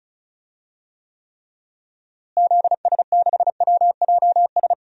Morse code, heard as a keyed tone, about 36 wpm, 700 Hz. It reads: ZS6WJS
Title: DXCC-100 Call Signs - 6 Characters - Repeated 3x - 35wpm